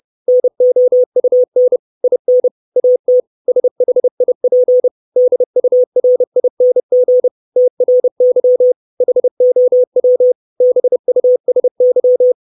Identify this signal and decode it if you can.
Morse code, heard as a keyed tone, about 30 wpm, 500 Hz. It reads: NOUN IN AT SHIP DURING TRY HOW BUSY
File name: 632 Words 30WPM